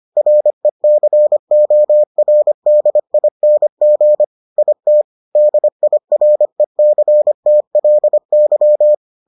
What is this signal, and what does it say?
Morse code, heard as a keyed tone, about 25 wpm, 600 Hz. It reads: RECORDING IT DIRECTLY